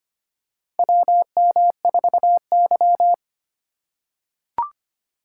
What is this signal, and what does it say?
Morse code, heard as a keyed tone, about 25 wpm, 700 Hz. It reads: WM4Y
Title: USA Call Signs - 3x Character Spacing - 25wpm